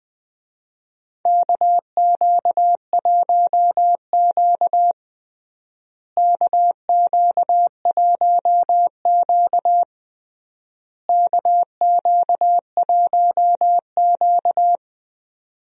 Morse code, heard as a keyed tone, about 20 wpm, 700 Hz. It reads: KQ1Q KQ1Q KQ1Q
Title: USA Call Signs - 4 Characters - Repeated 3x - 20wpm